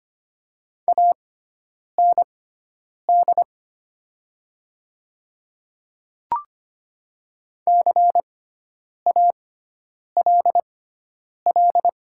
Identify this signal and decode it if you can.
Morse code, heard as a keyed tone, about 25 wpm, spaced as for 9 wpm, 700 Hz. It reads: AND CALL
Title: 6x Character Spacing - Top 100 Words - 25wpm